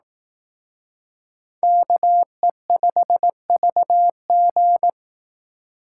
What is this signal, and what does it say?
Morse code, heard as a keyed tone, about 18 wpm, 700 Hz. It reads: KE5VG
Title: Speed Racing - USA 2x2 Call Signs 15wpm